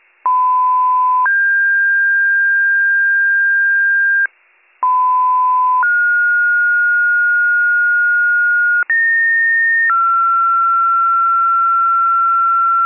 Two-tone page, heard as a one-second tone, then a three-second tone, about 1010 then 1600 Hz, about 1010 then 1430 Hz, about 1850 then 1360 Hz. It is heard from 0.2 to 12.9 s.